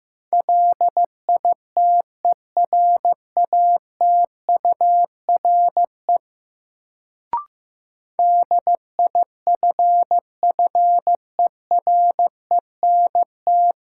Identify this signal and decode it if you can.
Morse code, heard as a keyed tone, about 15 wpm, 700 Hz. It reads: LITERATURE DIFFERENT
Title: Sentences from Top 600 Words - New Word Drill - 15wpm